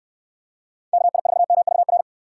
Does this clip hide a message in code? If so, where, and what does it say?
Yes: Morse code from 0.9 to 2.0 s, heard as a keyed tone, about 69 wpm, 700 Hz. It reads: KI4MVR